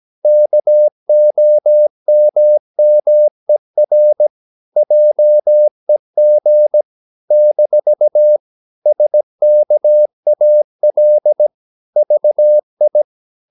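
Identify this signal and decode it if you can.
Morse code, heard as a keyed tone, about 17 wpm, 600 Hz. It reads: KOMMER JEG - SKAL VI
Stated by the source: Slow (QRS) Morse practice at 17 wpm, from CW med Gnister.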